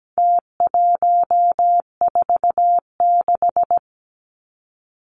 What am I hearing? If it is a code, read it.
Morse code, heard as a keyed tone, about 17 wpm, 700 Hz. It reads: T146
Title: Numbers - Contest Style - 4-Digit 17wpm